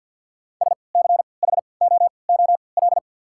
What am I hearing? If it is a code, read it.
Morse code, heard as a keyed tone, about 54 wpm, spaced as for 29 wpm, 700 Hz. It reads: SCHOOL
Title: 3x Character Spacing - Top 500 Words - 50wpm